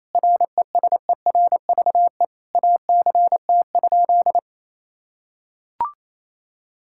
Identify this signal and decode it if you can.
Morse code, heard as a keyed tone, about 28 wpm, 700 Hz. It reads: RESERVE ACT?
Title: Sentences from Top 1000 Words - Review Part 6 - 28wpm